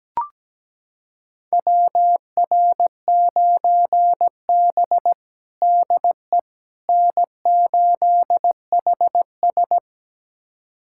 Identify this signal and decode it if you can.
Morse code, heard as a keyed tone, about 17 wpm, 700 Hz. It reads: WR9B DE N8HS
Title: USA Callsign DE Callsign - 4 Characters - 17wpm